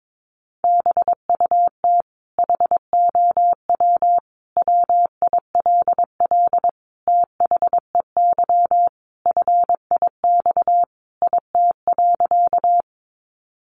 Morse code, heard as a keyed tone, about 22 wpm, 700 Hz. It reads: BUT HOW WILL THEY FIX IT.
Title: Sentences from Top 1400 Words 22wpm